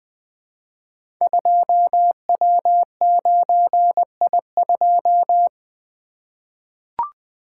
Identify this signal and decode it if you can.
Morse code, heard as a keyed tone, about 20 wpm, 700 Hz. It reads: 2W9I2
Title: Code Groups - 5 Letter:Numbers - 20wpm